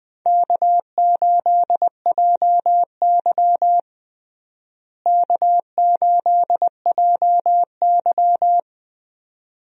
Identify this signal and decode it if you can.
Morse code, heard as a keyed tone, about 20 wpm, 700 Hz. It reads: K8JY K8JY
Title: USA Call Signs - 4 Characters - Repeated 2x - 20wpm